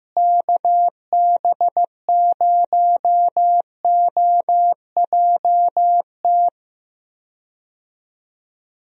Morse code, heard as a keyed tone, about 15 wpm, 700 Hz. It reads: KB0OJT
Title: USA Call Signs - 6 Characters - Repeated 2x - 15wpm